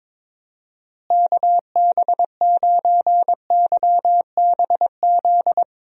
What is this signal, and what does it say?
Morse code, heard as a keyed tone, about 22 wpm, 700 Hz. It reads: KB9YBZ
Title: USA Call Signs - 6 Characters - Repeated 3x - 22wpm